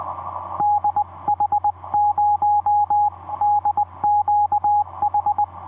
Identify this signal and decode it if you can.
Morse code, heard as a keyed tone, about 20 wpm, 830 Hz, with radio noise. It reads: DH0DQH
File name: v5-dk1ce dm1tde etc 17m cw 28dec2010 0942